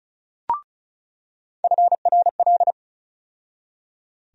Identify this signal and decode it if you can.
Morse code, heard as a keyed tone, about 36 wpm, 700 Hz. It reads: FRL
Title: USA Call Sign Postfixes - 3 Letters 35wpm